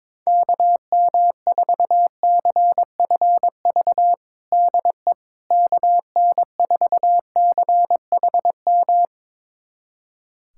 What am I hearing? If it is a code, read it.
Morse code, heard as a keyed tone, about 22 wpm, 700 Hz. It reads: KM4CFV DE KN4CHM
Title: USA Callsign DE Callsign - 6 Characters - 22wpm